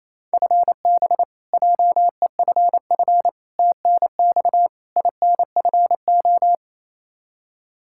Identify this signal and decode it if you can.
Morse code, heard as a keyed tone, about 28 wpm, 700 Hz. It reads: FB JEFF TNX INFO
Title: Practice QSO - Exchange 3 - More Info - 28wpm